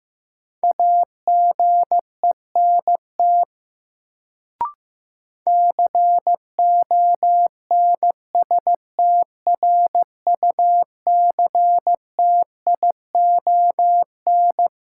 Morse code, heard as a keyed tone, about 15 wpm, 700 Hz. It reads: AGENT CONSTRUCTION
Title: Sentences from Top 1600 Words - New Word Drill Alt - 15wpm